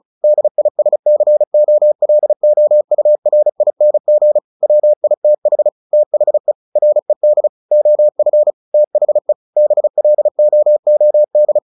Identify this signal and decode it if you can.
Morse code, heard as a keyed tone, about 34 wpm, 600 Hz. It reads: DISCOLOURING WITH THE RED OF THE BLOOD
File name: the old man and the sea 2 35wpm